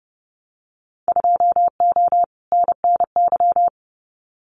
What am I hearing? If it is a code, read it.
Morse code, heard as a keyed tone, about 30 wpm, 700 Hz. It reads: 2O NNY
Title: Winter Field Day Exchanges 30wpm